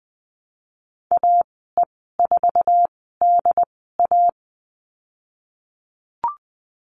Morse code, heard as a keyed tone, about 20 wpm, spaced as for 15 wpm, 700 Hz. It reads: AE4DA
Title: 069 Lesson 12g - Letter d - Call Signs 20wpm